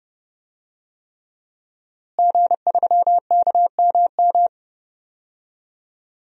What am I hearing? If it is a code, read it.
Morse code, heard as a keyed tone, about 30 wpm, 700 Hz. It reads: G3KMM
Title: DXCC-100 Call Signs - 5 Characters - Repeated 3x - 30wpm